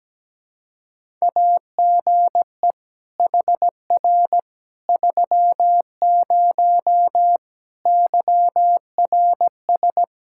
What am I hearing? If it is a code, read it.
Morse code, heard as a keyed tone, about 17 wpm, 700 Hz. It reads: AGE HR 30 YRS
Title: Practice QSO - Exchange 3 - More Info - 17wpm